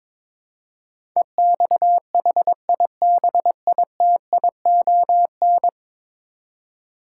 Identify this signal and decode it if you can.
Morse code, heard as a keyed tone, about 22 wpm, 700 Hz. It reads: EXHIBITION
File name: practice / 22wpm / 10 Letter Words - First 3 Letters Given - 22wpm